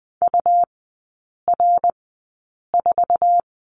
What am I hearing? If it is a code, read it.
Morse code, heard as a keyed tone, about 20 wpm, spaced as for 8 wpm, 700 Hz. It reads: UR4
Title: 138 Lesson 23e - Letter f - Three Characters 20wpm